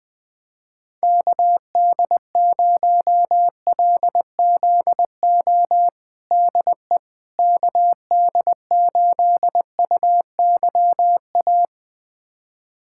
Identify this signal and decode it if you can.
Morse code, heard as a keyed tone, about 21 wpm, 700 Hz. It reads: KD0LZO DE KD8UYA
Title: USA Callsign DE Callsign - 6 Characters - 20wpm